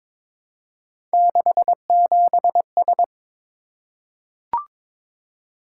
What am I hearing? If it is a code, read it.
Morse code, heard as a keyed tone, about 22 wpm, 700 Hz. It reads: 67S
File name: practice / 22wpm / Three Letter-Number Combinations - 22wpm